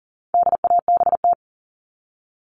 Morse code, heard as a keyed tone, about 40 wpm, 700 Hz. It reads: DABT
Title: All Cut Numbers - 4-Digit 40wpm